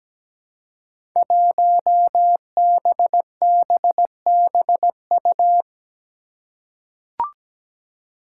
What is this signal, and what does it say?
Morse code, heard as a keyed tone, about 17 wpm, 700 Hz. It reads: 1BBBU
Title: Five Challenging Letter-Number Combinations 17wpm